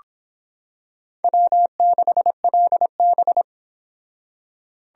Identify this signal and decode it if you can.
Morse code, heard as a keyed tone, about 26 wpm, 700 Hz. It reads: W6LB